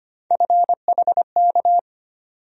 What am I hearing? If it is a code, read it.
Morse code, heard as a keyed tone, about 25 wpm, 700 Hz. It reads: FHK
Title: USA Call Sign Postfixes - 3 Letters 25wpm